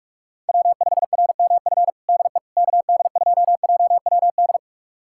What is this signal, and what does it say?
Morse code, heard as a keyed tone, about 46 wpm, 700 Hz. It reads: W5RMF DE KD2JWD
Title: USA Callsign DE Callsign 45wpm